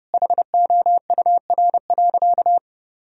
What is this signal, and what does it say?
Morse code, heard as a keyed tone, about 30 wpm, 700 Hz. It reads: HOUR.
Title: Sentences from Top 1000 Words - Review Part 1 - 30wpm